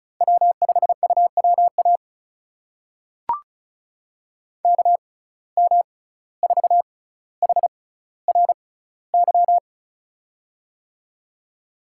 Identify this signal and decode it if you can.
Morse code, heard as a keyed tone, about 36 wpm, spaced as for 17 wpm, 700 Hz. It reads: W5UWA KM4HRY
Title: USA Call Signs - 6x Character Spacing - 35wpm